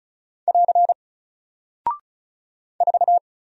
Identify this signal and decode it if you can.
Morse code, heard as a keyed tone, about 35 wpm, 700 Hz. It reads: <AR> 4
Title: Single Character (Letter:Number:Punctuation:Prosign) Alt 35wpm